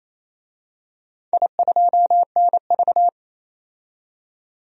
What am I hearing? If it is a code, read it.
Morse code, heard as a keyed tone, about 28 wpm, 700 Hz. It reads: I2NV
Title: DXCC-100 Call Signs - 4 Characters - 28wpm